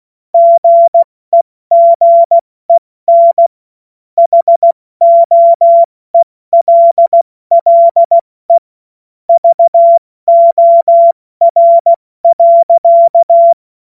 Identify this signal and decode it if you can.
Morse code, heard as a keyed tone, about 16 wpm, spaced as for 14 wpm, 680 Hz. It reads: GEGEN HOELLE VOR.
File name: Morsekurs / Witze / Witz 02. Februar 2021, 70 F80